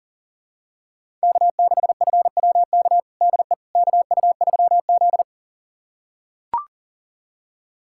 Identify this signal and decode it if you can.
Morse code, heard as a keyed tone, about 40 wpm, 700 Hz. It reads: K6FWK DE KU3Z
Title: USA Callsign DE Callsign 40wpm